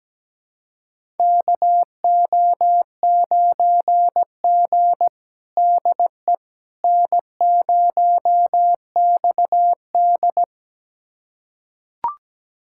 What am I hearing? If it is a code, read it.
Morse code, heard as a keyed tone, about 17 wpm, 700 Hz. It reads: KO9G DE N0XD
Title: USA Callsign DE Callsign - 4 Characters - 17wpm